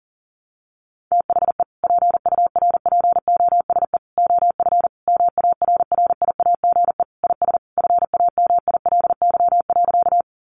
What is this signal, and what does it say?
Morse code, heard as a keyed tone, about 40 wpm, 690 Hz. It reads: THE PURPOSE OF MARRIAGE IS FAMILY.
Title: Sentences from Top 800 Words 40wpm